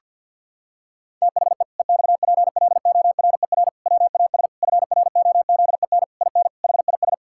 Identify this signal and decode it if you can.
Morse code, heard as a keyed tone, about 50 wpm, 700 Hz. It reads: THE EXPLORER WAS FROZEN IN HIS
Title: Pangram Challenge v2 - 50wpm